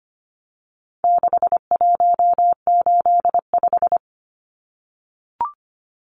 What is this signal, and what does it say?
Morse code, heard as a keyed tone, about 25 wpm, 700 Hz. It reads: 6185
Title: Numbers - 4-Digit 25wpm